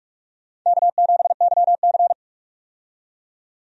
Morse code, heard as a keyed tone, about 47 wpm, 700 Hz. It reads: K7YC
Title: USA Call Signs - 4 Characters - 45wpm